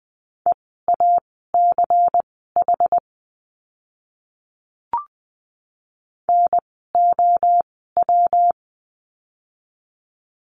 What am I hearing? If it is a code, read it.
Morse code, heard as a keyed tone, about 20 wpm, spaced as for 15 wpm, 700 Hz. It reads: EACH NOW
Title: 192 Lesson 31f - Letter v - Words 20wpm